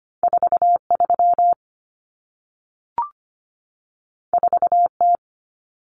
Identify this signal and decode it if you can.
Morse code, heard as a keyed tone, about 25 wpm, 700 Hz. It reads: 43 4T
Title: Common Cut Numbers - 2-Digit 25wpm